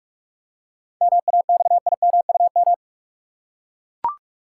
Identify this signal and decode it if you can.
Morse code, heard as a keyed tone, about 43 wpm, 700 Hz. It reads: MAXIMUM
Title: Seven Letter Words - 45wpm